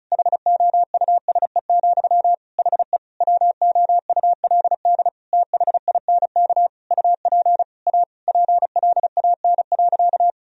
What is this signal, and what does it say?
Morse code, heard as a keyed tone, about 36 wpm, 700 Hz. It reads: HOUSE, HE WOULD THINK UP A PLAN.
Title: Sentences from Top 1000 Words - Review Part 2 - 35wpm